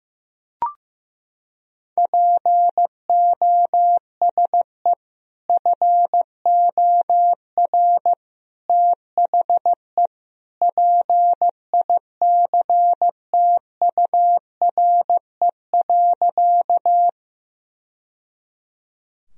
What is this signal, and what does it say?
Morse code, heard as a keyed tone, about 15 wpm, 700 Hz. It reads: POSE FOR THE PICTURE.
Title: Sentences from Top 1000 Words - Review Part 2 - 15wpm